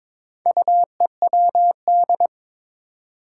Morse code, heard as a keyed tone, about 21 wpm, 700 Hz. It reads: UEWD
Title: Four Challenging Letter-Number Combinations 22wpm